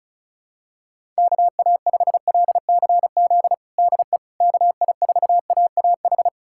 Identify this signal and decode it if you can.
Morse code, heard as a keyed tone, about 36 wpm, 700 Hz. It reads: KA5LCZ DE KI4AAH
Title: USA Callsign DE Callsign - 6 Characters - 35wpm